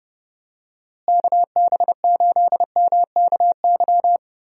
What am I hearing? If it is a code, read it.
Morse code, heard as a keyed tone, about 30 wpm, 700 Hz. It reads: KB8MKY
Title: USA Call Signs - 6 Characters - Repeated 3x - 30wpm